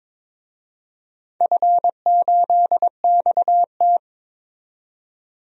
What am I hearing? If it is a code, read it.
Morse code, heard as a keyed tone, about 22 wpm, 690 Hz. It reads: F8XT
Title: DXCC-100 Call Signs - 4 Characters - 22wpm